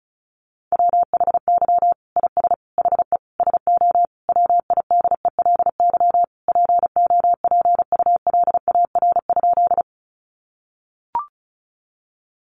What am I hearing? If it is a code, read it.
Morse code, heard as a keyed tone, about 34 wpm, 700 Hz. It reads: WHY IS HE SO WIDELY POPULAR?
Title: Sentences from Top 1200 Words 35wpm